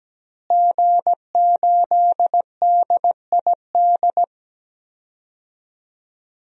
Morse code, heard as a keyed tone, about 17 wpm, 700 Hz. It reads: G8DID
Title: DXCC-100 Call Signs - 5 Characters - Repeated 3x - 17wpm